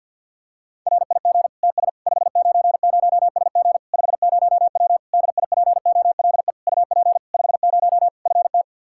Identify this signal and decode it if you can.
Morse code, heard as a keyed tone, about 50 wpm, 700 Hz. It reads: RIG TS 590SG 50W DIPOLE UP 50 FT
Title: Practice QSO - Exchange 2 - Info - 50wpm